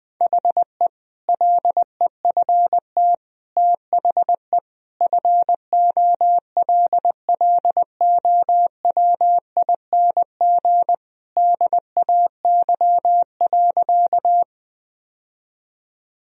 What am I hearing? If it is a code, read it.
Morse code, heard as a keyed tone, about 20 wpm, 700 Hz. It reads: HE LEFT THE FOLLOWING DAY.
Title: Sentences from Top 1000 Words - Review Part 4 - 20wpm